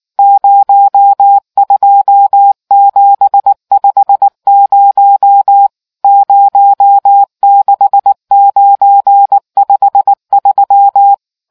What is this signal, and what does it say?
Morse code, heard as a keyed tone, about 19 wpm, 800 Hz. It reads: 02750 06953